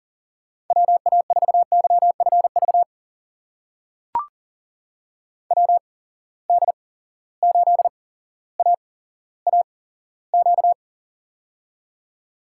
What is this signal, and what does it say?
Morse code, heard as a keyed tone, about 37 wpm, spaced as for 17 wpm, 700 Hz. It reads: WA4YFV WD8AAQ
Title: USA Call Signs - 6 Characters - 8x Character Spacing - 40wpm